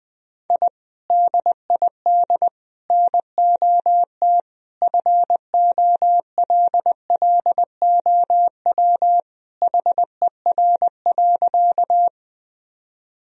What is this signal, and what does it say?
Morse code, heard as a keyed tone, about 21 wpm, 700 Hz. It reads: I DID NOT FOLLOW HER.
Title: Sentences from Top 1000 Words - Review Part 1 - 20wpm